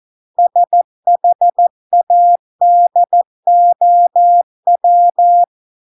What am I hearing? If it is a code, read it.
Morse code, heard as a keyed tone, about 14 wpm, 700 Hz. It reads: SHADOW